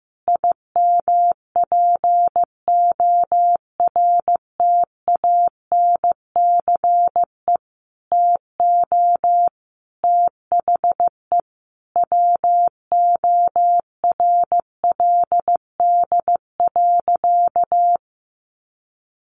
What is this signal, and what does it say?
Morse code, heard as a keyed tone, about 15 wpm, 700 Hz. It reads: IMPORTANCE TO THE WORLD.
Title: Sentences from Top 1200 Words 15wpm